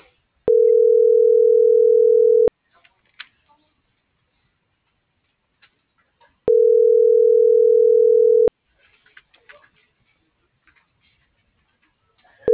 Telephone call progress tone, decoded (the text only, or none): ringback tone